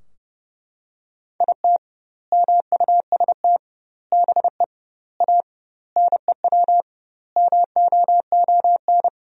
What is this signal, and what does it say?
Morse code, heard as a keyed tone, about 32 wpm, 700 Hz. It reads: IT MUST BE A NEW MOON
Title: 2x Word-Spacing - Sentences from Top 500 Words - 30wpm